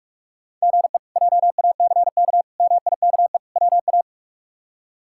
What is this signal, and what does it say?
Morse code, heard as a keyed tone, about 44 wpm, 700 Hz. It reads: GE JACK MIKE WA
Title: K1USN SST QSO Practice - 45wpm